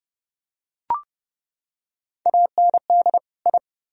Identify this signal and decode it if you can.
Morse code, heard as a keyed tone, about 28 wpm, 700 Hz. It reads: AND I